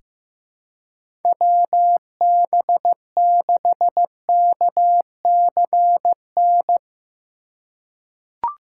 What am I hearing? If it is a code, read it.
Morse code, heard as a keyed tone, about 15 wpm, 700 Hz. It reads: WB6KCN
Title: USA Call Signs - 6 Characters - 6x Character Spacing - 15wpm